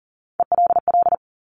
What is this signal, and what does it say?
Morse code, heard as a keyed tone, about 42 wpm, 690 Hz. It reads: ELL
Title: Top Three Letter Combinations in Words 40wpm